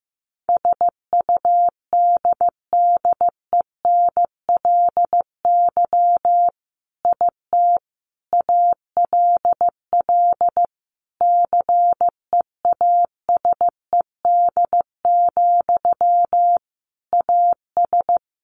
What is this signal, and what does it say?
Morse code, heard as a keyed tone, about 15 wpm, 700 Hz. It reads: SUDDENLY IT ALL CEASED, AS
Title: A Sound Of Thunder 15wpm